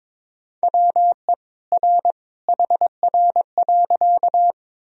Morse code, heard as a keyed tone, about 22 wpm, 700 Hz. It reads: WE R HR.
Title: Sentences from Top 200 Words and Abbreviations 22wpm